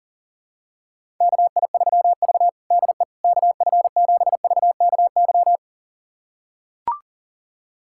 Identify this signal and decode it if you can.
Morse code, heard as a keyed tone, about 40 wpm, 700 Hz. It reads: KI3V DE KF7VKY
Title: USA Callsign DE Callsign 40wpm